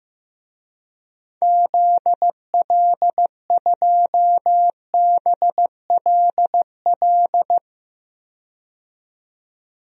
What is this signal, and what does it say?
Morse code, heard as a keyed tone, about 15 wpm, 700 Hz. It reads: ZL2BLL
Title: DXCC-100 Call Signs - 6 Characters - 15wpm